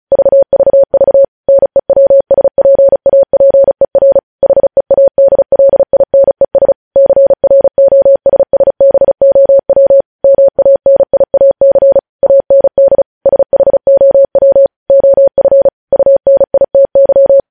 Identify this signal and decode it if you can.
Morse code, heard as a keyed tone, about 36 wpm, 550 Hz. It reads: VVV NEWSPAPER HEADLINES CROSSBOW MANIAC AND SHOW OF UNITY
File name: podcast / News headlines-1x-35x35